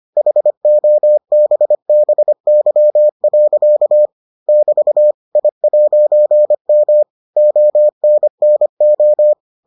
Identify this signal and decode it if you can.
Morse code, heard as a keyed tone, about 25 wpm, 600 Hz. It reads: HOBBY. <BT> I'M ONNO